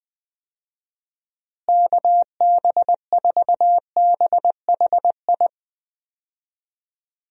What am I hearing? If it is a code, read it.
Morse code, heard as a keyed tone, about 20 wpm, 700 Hz. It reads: KB4BHI
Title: USA Call Signs - 6 Characters - Repeated 2x - 20wpm